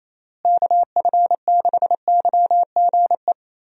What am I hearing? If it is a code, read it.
Morse code, heard as a keyed tone, about 28 wpm, 700 Hz. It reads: KF6YGE
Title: USA Call Signs - 6 Characters - Repeated 2x - 28wpm